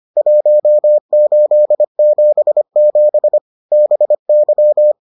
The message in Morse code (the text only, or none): 1877 BY